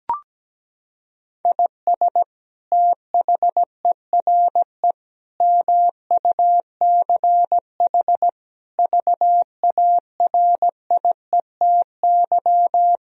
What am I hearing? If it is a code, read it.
Morse code, heard as a keyed tone, about 17 wpm, 700 Hz. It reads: IS THERE MUCH VARIETY
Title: Sentences from Top 1000 Words - Review Part 4 - 17wpm